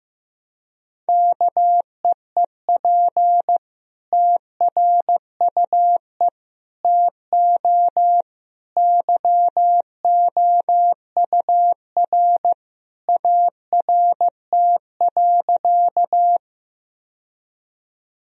Morse code, heard as a keyed tone, about 15 wpm, 700 Hz. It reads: KEEP TRUE TO YOUR ART.
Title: Sentences from Top 1000 Words - Review Part 3 - 15wpm